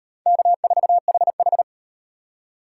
Morse code, heard as a keyed tone, about 38 wpm, 700 Hz. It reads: K4HH